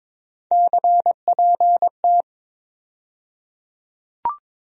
Morse code, heard as a keyed tone, about 22 wpm, 700 Hz. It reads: CPT
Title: USA Call Sign Postfixes - 3 Letters 22wpm